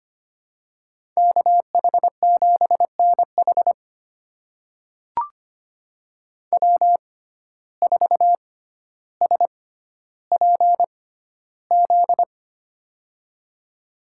Morse code, heard as a keyed tone, about 25 wpm, spaced as for 13 wpm, 700 Hz. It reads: KH7NH W4SPZ
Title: USA Call Signs - 6x Character Spacing - 25wpm